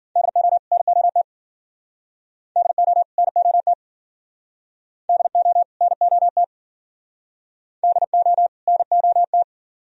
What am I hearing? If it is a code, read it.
Morse code, heard as a keyed tone, about 53 wpm, 700 Hz. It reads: DO NOT DO NOT DO NOT DO NOT